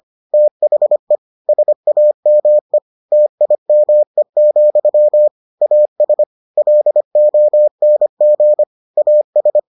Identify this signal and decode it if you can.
Morse code, heard as a keyed tone, about 25 wpm, 600 Hz. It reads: THE SAME TIME, AS LONG AS